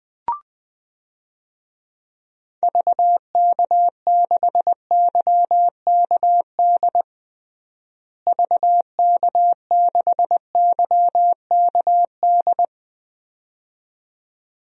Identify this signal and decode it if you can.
Morse code, heard as a keyed tone, about 20 wpm, 700 Hz. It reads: VK6YKD VK6YKD
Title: DXCC-100 Call Signs - 6 Characters - Repeated 2x - 20wpm